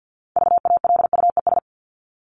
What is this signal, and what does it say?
Morse code, heard as a keyed tone, about 50 wpm, 700 Hz. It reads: VALUES
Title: ES:S Suffix - Top 100 Words with Base Words 50wpm